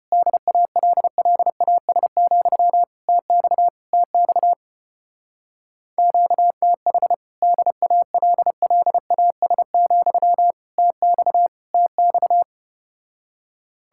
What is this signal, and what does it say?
Morse code, heard as a keyed tone, about 33 wpm, 700 Hz. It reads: DALLAS, TX TX QTH DALLAS, TX TX